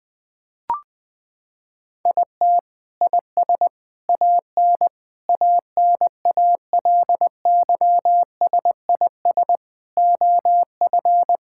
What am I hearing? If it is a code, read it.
Morse code, heard as a keyed tone, about 20 wpm, 700 Hz. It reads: IT IS AN ANALYSIS OF
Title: Sentences from Top 1000 Words - Review Part 5 - 20wpm